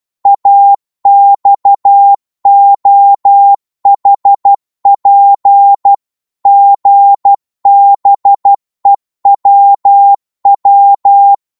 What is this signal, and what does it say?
Morse code, heard as a keyed tone, about 12 wpm, 800 Hz. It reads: AXOHP GBEWW
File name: Morse / ltr-12wpm